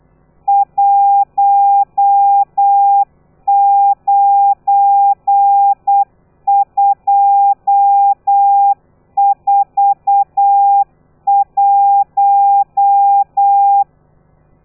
Morse code, heard as a keyed tone, about 8 wpm, 800 Hz. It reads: 19241